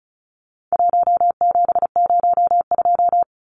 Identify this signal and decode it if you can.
Morse code, heard as a keyed tone, about 34 wpm, 700 Hz. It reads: 1702
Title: Numbers - 4-Digit 35wpm